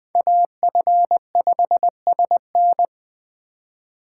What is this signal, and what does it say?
Morse code, heard as a keyed tone, about 19 wpm, 700 Hz. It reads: AF5SN